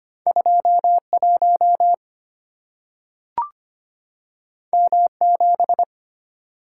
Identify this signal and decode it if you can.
Morse code, heard as a keyed tone, about 25 wpm, 700 Hz. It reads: 21 M7
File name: Two Letter-Number Combinations - 25wpm